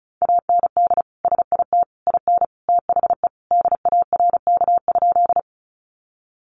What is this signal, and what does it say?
Morse code, heard as a keyed tone, about 34 wpm, 700 Hz. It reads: AND SIT IN THE DARK?
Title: Sentences from Top 500 Words 35wpm